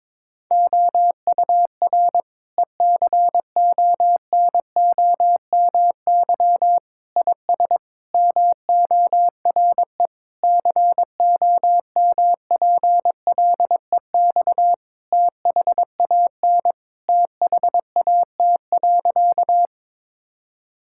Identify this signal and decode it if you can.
Morse code, heard as a keyed tone, about 22 wpm, 700 Hz. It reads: OUR ECONOMY IS MORE COMPLEX THAN THAT.
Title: Sentences from Top 1000 Words - Review Part 6 - 22wpm